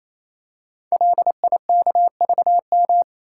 Morse code, heard as a keyed tone, about 27 wpm, 700 Hz. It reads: LIKVM
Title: Code Groups - 5 Letters - 28wpm